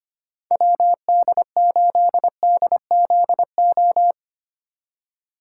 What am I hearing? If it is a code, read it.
Morse code, heard as a keyed tone, about 25 wpm, 700 Hz. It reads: WD8DZO
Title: USA Call Signs - 4x Character Spacing - 25wpm